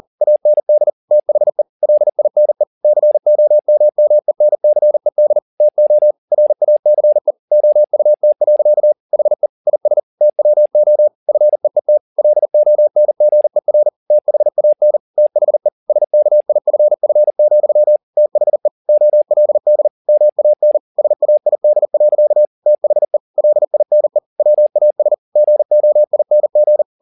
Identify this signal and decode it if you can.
Morse code, heard as a keyed tone, about 40 wpm, 600 Hz. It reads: AND THE LINE COMMENCED TO RACE OUT. HE IS TWO FEET LONGER THAN THE SKIFF, THE OLD MAN SAID. THE LINE WAS GOING